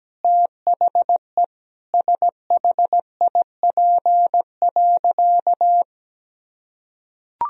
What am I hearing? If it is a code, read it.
Morse code, heard as a keyed tone, about 17 wpm, 700 Hz. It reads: THE SHIP.
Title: Sentences from Top 1000 Words - Review Part 2 - 17wpm